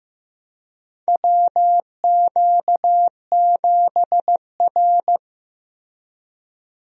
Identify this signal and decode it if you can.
Morse code, heard as a keyed tone, about 15 wpm, 700 Hz. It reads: WQ7R
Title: USA Call Signs - 4 Characters - Repeated 3x - 15wpm